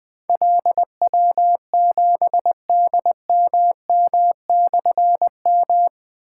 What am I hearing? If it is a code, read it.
Morse code, heard as a keyed tone, about 20 wpm, 700 Hz. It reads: LW7DMM/M
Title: Difficult Call Signs 20wpm